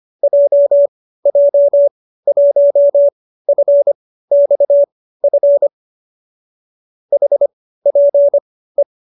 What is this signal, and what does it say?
Morse code, heard as a keyed tone, about 25 wpm, spaced as for 15 wpm, 570 Hz. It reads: JJ1FXF HPE